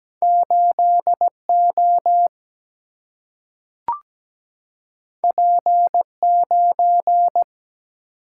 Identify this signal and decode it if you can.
Morse code, heard as a keyed tone, about 17 wpm, 700 Hz. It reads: 8O P9
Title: Single-Letter Single-Number 17wpm